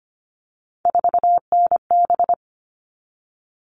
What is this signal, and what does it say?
Morse code, heard as a keyed tone, about 25 wpm, 700 Hz. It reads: 4NB
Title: All Cut Numbers - 3-Digit 25wpm